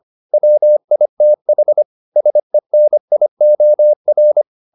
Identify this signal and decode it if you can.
Morse code, heard as a keyed tone, about 25 wpm, 600 Hz. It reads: WITH SENIOR